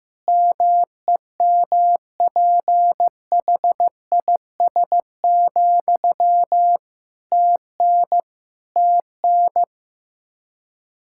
Morse code, heard as a keyed tone, about 15 wpm, 700 Hz. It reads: MEMPHIS, TN TN
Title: Practice QSO - Exchange 1 - Beginning - 15wpm